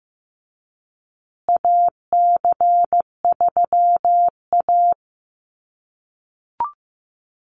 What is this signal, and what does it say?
Morse code, heard as a keyed tone, about 15 wpm, 700 Hz. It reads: AC3A
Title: USA Call Signs Repeated 3x 15wpm